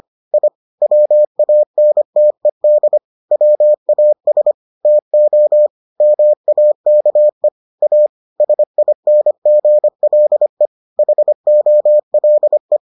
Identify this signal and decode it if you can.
Morse code, heard as a keyed tone, about 25 wpm, 600 Hz. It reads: I WANTED WAS TO MAKE A SINGLE HOLE